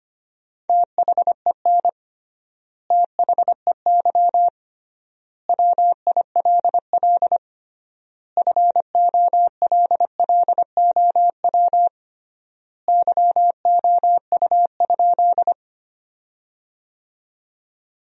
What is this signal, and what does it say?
Morse code, heard as a keyed tone, about 25 wpm, 700 Hz. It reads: THEN THEY WILL FOLLOW YOU?
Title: 3x Word-Spacing - Sentences from Top 200 Words - 25wpm